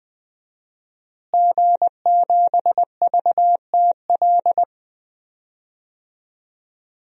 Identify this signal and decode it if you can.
Morse code, heard as a keyed tone, about 20 wpm, 700 Hz. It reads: G7VTL
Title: DXCC-100 Call Signs - 5 Characters - Repeated 3x - 20wpm